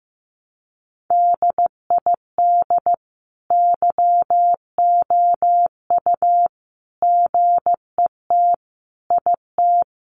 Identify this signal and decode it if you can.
Morse code, heard as a keyed tone, about 15 wpm, 700 Hz. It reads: DID YOU GET IT
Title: Sentences from Top 1300 Words 15wpm